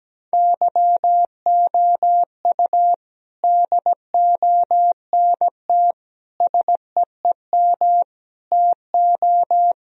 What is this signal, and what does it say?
Morse code, heard as a keyed tone, about 17 wpm, 700 Hz. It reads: YOU DONT SEEM TO
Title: Sentences from Top 1000 Words - Review Part 1 - 17wpm